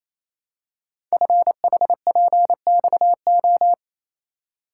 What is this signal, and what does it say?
Morse code, heard as a keyed tone, about 28 wpm, 700 Hz. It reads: FHPXO
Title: Code Groups - 5 Letters - 28wpm